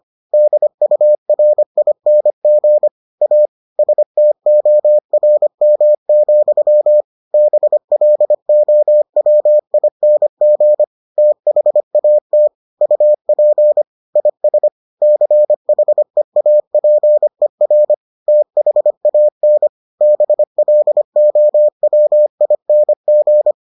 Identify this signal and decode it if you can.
Morse code, heard as a keyed tone, about 25 wpm, 600 Hz. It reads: DURING A STORM, BLOWING THAT UP IS CHEAPER THAN BLOWING